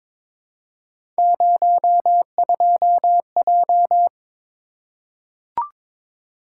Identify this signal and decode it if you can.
Morse code, heard as a keyed tone, about 22 wpm, 700 Hz. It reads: 02J
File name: practice / 22wpm / Three Letter-Number Combinations - 22wpm